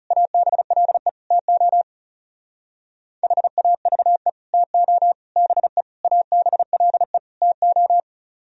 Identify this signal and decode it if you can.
Morse code, heard as a keyed tone, about 37 wpm, 700 Hz. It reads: ABLE TO HAVE TO BE ABLE TO
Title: Speed-Racing - Sets of 5 Words 30wpm